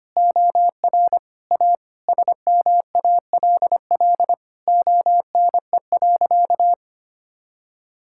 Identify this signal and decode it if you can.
Morse code, heard as a keyed tone, about 25 wpm, 700 Hz. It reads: OR A SMALL ONE.
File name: practice / 25wpm / Sentences with 2-Word Phrases - 25wpm